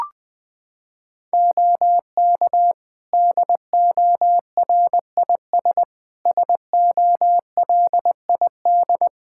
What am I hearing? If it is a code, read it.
Morse code, heard as a keyed tone, about 21 wpm, 700 Hz. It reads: OK DORIS SOLID
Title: Practice QSO - Exchange 3 - More Info - 20wpm